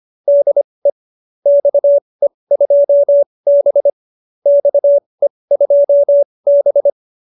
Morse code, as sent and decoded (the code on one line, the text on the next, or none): -.. / -..- . ..--- -... / -..- . ..--- -...
DE XE2B XE2B